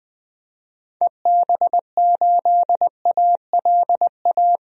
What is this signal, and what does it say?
Morse code, heard as a keyed tone, about 20 wpm, 700 Hz. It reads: EB8ALA
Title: DXCC-100 Call Signs - 6 Characters - 20wpm